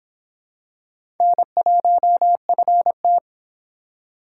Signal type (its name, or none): Morse code